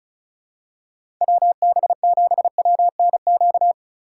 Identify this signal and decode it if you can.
Morse code, heard as a keyed tone, about 36 wpm, 700 Hz. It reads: WB7WNQ
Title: USA Call Signs - 2x Character Spacing - 35wpm